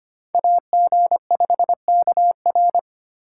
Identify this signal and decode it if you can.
Morse code, heard as a keyed tone, about 33 wpm, 700 Hz. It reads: AG5KR